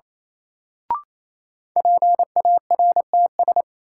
Morse code, heard as a keyed tone, about 28 wpm, 700 Hz. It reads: PARTS
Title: Sets of 3 Words - New Word Drill - 28wpm